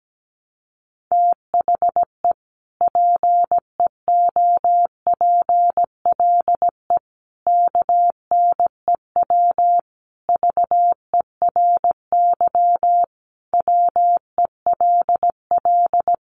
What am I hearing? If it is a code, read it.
Morse code, heard as a keyed tone, about 17 wpm, 700 Hz. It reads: THE PEOPLE KNEW VERY WELL
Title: Sentences from Top 1100 Words 17wpm